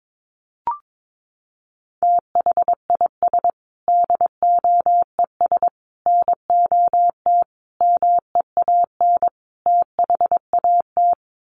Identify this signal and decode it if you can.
Morse code, heard as a keyed tone, about 22 wpm, 700 Hz. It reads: THIS DOES NOT MEAN THAT
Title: Sets of 5 Words Encore2 22wpm